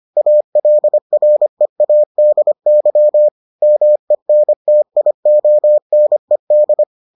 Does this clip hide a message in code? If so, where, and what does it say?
Yes: Morse code from 0.2 to 6.8 s, heard as a keyed tone, about 25 wpm, 600 Hz. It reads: ALREADY MENTIONED